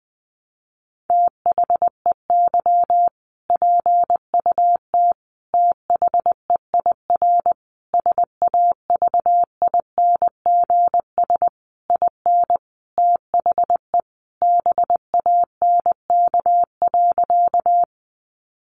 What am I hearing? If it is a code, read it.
Morse code, heard as a keyed tone, about 20 wpm, 700 Hz. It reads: THEY PUT THEIR SAVINGS IN THE BANK.
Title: Sentences from Top 1200 Words 20wpm